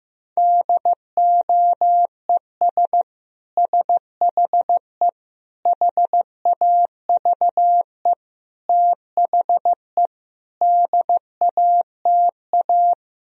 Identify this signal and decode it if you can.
Morse code, heard as a keyed tone, about 15 wpm, 700 Hz. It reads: DOES SHE HAVE THE DATA
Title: Sentences from Top 1000 Words - Review Part 3 - 15wpm